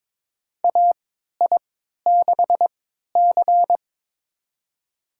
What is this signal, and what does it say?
Morse code, heard as a keyed tone, about 22 wpm, spaced as for 12 wpm, 700 Hz. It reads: AI6C
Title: USA Call Signs - 3x Character Spacing - 22wpm